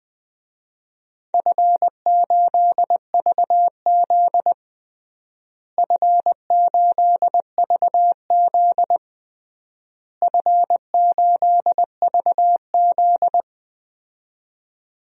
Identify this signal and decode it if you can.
Morse code, heard as a keyed tone, about 20 wpm, 700 Hz. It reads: F8VZ F8VZ F8VZ
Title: DXCC-100 Call Signs - 4 Characters - Repeated 3x - 20wpm